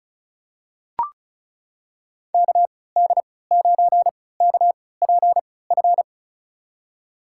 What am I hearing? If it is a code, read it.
Morse code, heard as a keyed tone, about 34 wpm, spaced as for 20 wpm, 700 Hz. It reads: KD9KPF
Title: USA Call Signs - 6 Characters - 3x Character Spacing - 35wpm